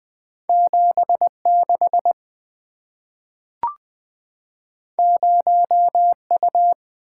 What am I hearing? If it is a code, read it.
Morse code, heard as a keyed tone, about 20 wpm, 700 Hz. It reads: 76 0U
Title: Two Letter-Number Combinations - 20wpm